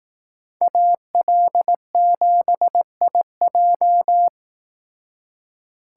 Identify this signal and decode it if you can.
Morse code, heard as a keyed tone, about 18 wpm, 700 Hz. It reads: AL7IJ
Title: Speed Racing - USA Call Signs 15wpm